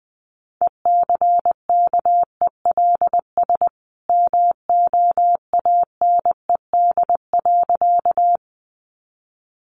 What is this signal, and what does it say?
Morse code, heard as a keyed tone, about 20 wpm, 700 Hz. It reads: ECKELS MOANED.
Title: A Sound Of Thunder 20wpm